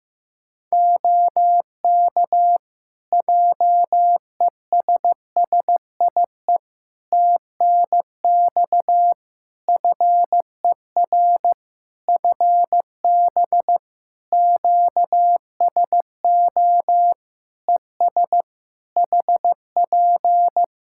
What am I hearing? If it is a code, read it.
Morse code, heard as a keyed tone, about 15 wpm, 700 Hz. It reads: OK JESSIE TNX FER FB QSO ES HP
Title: Practice QSO - Exchange 4 - Ending - 15wpm